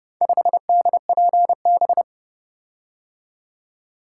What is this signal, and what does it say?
Morse code, heard as a keyed tone, about 29 wpm, 700 Hz. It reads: 5DPB